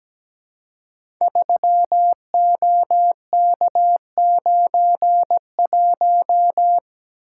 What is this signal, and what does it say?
Morse code, heard as a keyed tone, about 17 wpm, 700 Hz. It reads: 3OK91
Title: Code Groups - 5 Letter:Numbers - 17wpm